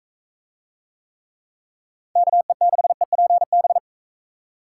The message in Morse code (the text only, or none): KE6EPB